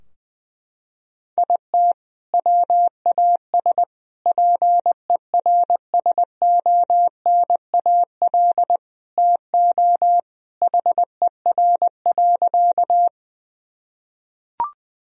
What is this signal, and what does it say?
Morse code, heard as a keyed tone, about 21 wpm, 700 Hz. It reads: IT WAS PERSONAL TO HER.
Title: Sentences from Top 1000 Words - Review Part 4 - 20wpm